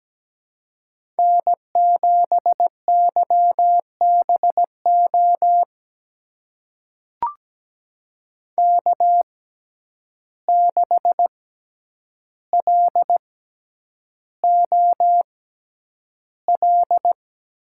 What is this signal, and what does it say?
Morse code, heard as a keyed tone, about 17 wpm, spaced as for 9 wpm, 700 Hz. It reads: N7YBO K6LOL
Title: USA Call Signs - 5 Characters - 6x Character Spacing - 17wpm